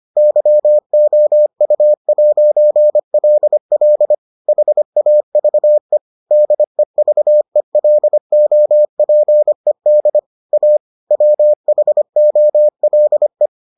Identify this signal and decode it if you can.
Morse code, heard as a keyed tone, about 25 wpm, 600 Hz. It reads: YOU'LL HAVE DEVELOPED A WHOLE